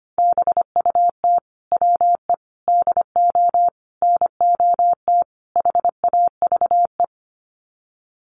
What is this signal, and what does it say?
Morse code, heard as a keyed tone, about 25 wpm, 700 Hz. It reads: BUT WE DO NOT HAVE
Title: Sets of 5 Words Encore2 25wpm